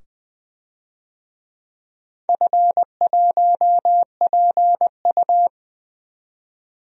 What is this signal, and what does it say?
Morse code, heard as a keyed tone, about 20 wpm, 700 Hz. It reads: F1PU
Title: DXCC-100 Call Signs - 4 Characters - 20wpm